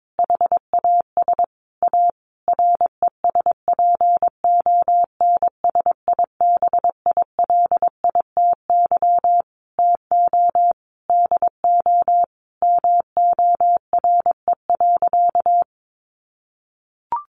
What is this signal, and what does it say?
Morse code, heard as a keyed tone, about 22 wpm, 700 Hz. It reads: HAS A RESPONSIBILITY TO DO MORE.
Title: Sentences from Top 1400 Words 22wpm